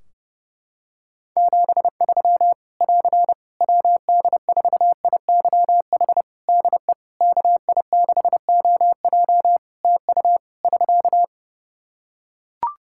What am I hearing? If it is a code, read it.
Morse code, heard as a keyed tone, about 30 wpm, 700 Hz. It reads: 73 <AR> WD4IYH DE KI6OJ TU <SK>
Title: K1USN SST QSO Practice 30wpm